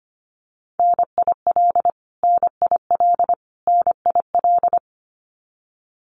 Morse code, heard as a keyed tone, about 25 wpm, 700 Hz. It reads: NIL NIL NIL
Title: Abbreviations Repeated 3x - 25wpm